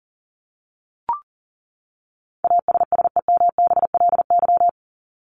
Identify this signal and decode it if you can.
Morse code, heard as a keyed tone, about 40 wpm, 700 Hz. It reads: ASSEMBLY
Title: LY Suffix - Top 100 Words with Base Words 40wpm